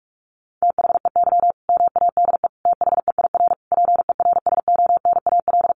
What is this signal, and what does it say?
Morse code, heard as a keyed tone, about 44 wpm, 700 Hz. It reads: THEY MADE THEIR PERSONAL
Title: Sentences from Top 1100 Words 45wpm